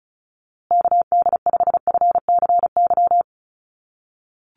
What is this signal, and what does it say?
Morse code, heard as a keyed tone, about 34 wpm, 700 Hz. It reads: KD5FCY
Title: USA Call Signs Repeated 2x 35wpm